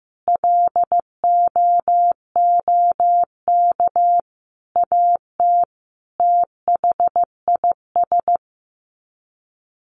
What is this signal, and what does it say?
Morse code, heard as a keyed tone, about 15 wpm, 700 Hz. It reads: LOOK AT THIS
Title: Sets of 5 Words 15wpm